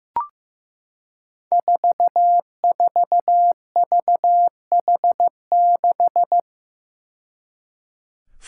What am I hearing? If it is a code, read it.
Morse code, heard as a keyed tone, about 15 wpm, 700 Hz. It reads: 44VH6
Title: Five Challenging Letter-Number Combinations 15wpm